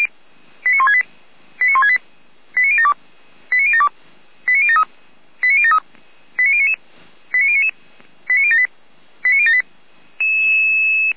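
Five-tone selective call (ZVEI), heard as quick runs of five tones, tones of about 70 ms. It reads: 89168 89168 89072 89072 89073 89073 89099 89099 89078 89078
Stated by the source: ZVEI five-tone selective call 89150, received with an SDR.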